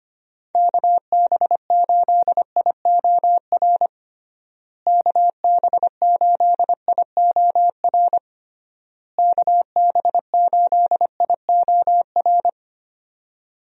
Morse code, heard as a keyed tone, about 25 wpm, 700 Hz. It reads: KB8IOR KB8IOR KB8IOR
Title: USA Call Signs - 6 Characters - Repeated 3x - 25wpm